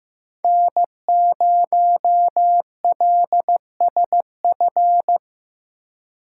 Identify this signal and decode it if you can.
Morse code, heard as a keyed tone, about 17 wpm, 700 Hz. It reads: N0LSF
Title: Speed Racing - USA 1x3 Call Signs 15wpm